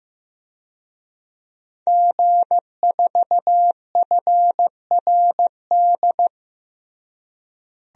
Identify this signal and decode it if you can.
Morse code, heard as a keyed tone, about 15 wpm, 700 Hz. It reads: G4FRD
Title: DXCC-100 Call Signs - 5 Characters - Repeated 2x - 15wpm